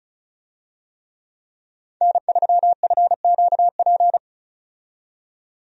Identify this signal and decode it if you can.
Morse code, heard as a keyed tone, about 36 wpm, 700 Hz. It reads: N3FQP
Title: USA Call Signs - 5 Characters - 35wpm